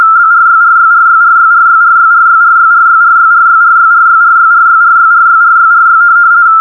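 RTTY, about 50 baud. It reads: BRYRYRYRYRYRYRYRYRYRYRYRYRYRYRYRYRYRYRY
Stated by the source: RTTY test tape from Hamburg Meteo (DWD, Deutscher Wetter Dienst) telegraphic weather service on 147.3, recorded from longwave with only 85 Hz shift.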